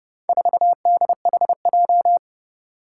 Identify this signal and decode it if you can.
Morse code, heard as a keyed tone, about 30 wpm, 700 Hz. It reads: 4DHJ
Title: Four Challenging Letter-Number Combinations 30wpm